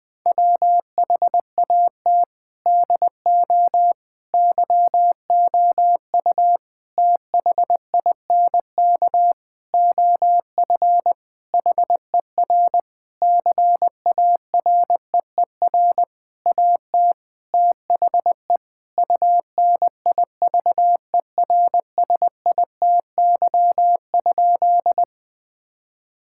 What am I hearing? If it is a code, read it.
Morse code, heard as a keyed tone, about 21 wpm, 700 Hz. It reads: WHAT DO YOU THINK OF HER CAREER AT THE UNIVERSITY?
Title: Sentences from Top 1000 Words - Review Part 6 - 20wpm